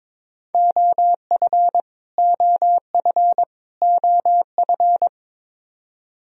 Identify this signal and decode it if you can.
Morse code, heard as a keyed tone, about 21 wpm, 700 Hz. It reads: OF OF OF
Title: Top 100 Words Repeated 22wpm